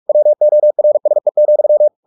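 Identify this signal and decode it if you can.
Morse code, heard as a keyed tone, about 44 wpm, 600 Hz. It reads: WORSE,
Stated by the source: Morse practice at 45 wpm, the novel Never by Ken Follett in English, from CW med Gnister.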